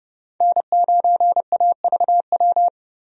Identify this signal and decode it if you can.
Morse code, heard as a keyed tone, about 32 wpm, 700 Hz. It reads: N9AVW
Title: USA Call Signs - 5 Characters - 30wpm